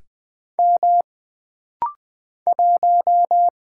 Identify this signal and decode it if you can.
Morse code, heard as a keyed tone, about 20 wpm, 700 Hz. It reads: M 1
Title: Single Letter-Number - Rapid-Fire 20wpm